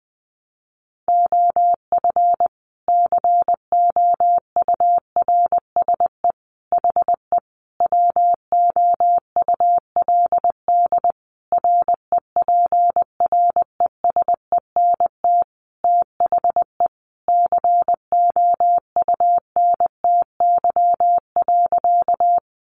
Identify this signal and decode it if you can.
Morse code, heard as a keyed tone, about 20 wpm, 700 Hz. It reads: OF COURSE HE WOULD REPRESENT THE COUNTY.
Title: Sentences from Top 1200 Words 20wpm